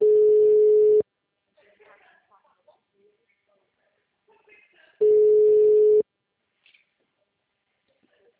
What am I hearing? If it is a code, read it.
Ringback tone (Europe, 425 Hz), with speech.